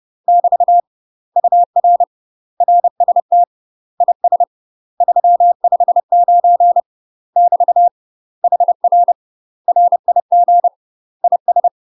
Morse code, heard as a keyed tone, about 30 wpm, 700 Hz. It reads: <BT> UR RST IS 359 <BT> HR RIG IS
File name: UK MASTER QSOs TRAINING / 30WPM / UK MASTER QSOs 0009